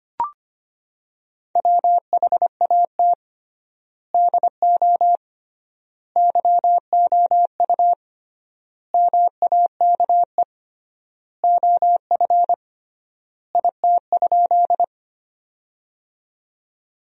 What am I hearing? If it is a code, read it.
Morse code, heard as a keyed tone, about 25 wpm, 700 Hz. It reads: WHAT DO YOU MAKE OF IT?
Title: 3x Word-Spacing - Sentences from Top 100 Words - 25wpm